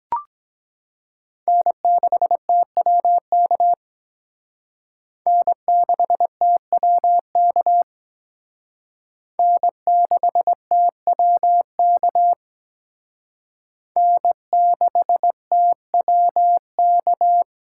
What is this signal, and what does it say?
Morse code, heard as a keyed tone, about 22 wpm, 700 Hz. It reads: N6TWK N6TWK N6TWK N6TWK